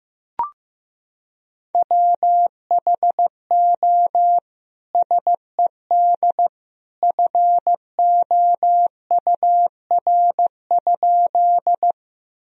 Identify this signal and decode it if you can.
Morse code, heard as a keyed tone, about 15 wpm, 700 Hz. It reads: WHO SED FOUR?
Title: Sentences from Top 300 Words and Abbreviations 15wpm